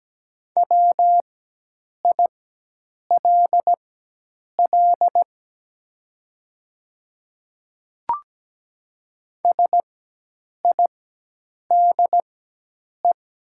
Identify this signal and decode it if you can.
Morse code, heard as a keyed tone, about 17 wpm, spaced as for 8 wpm, 700 Hz. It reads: WILL SIDE
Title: 4x Character Spacing - Top 200 Words - 17wpm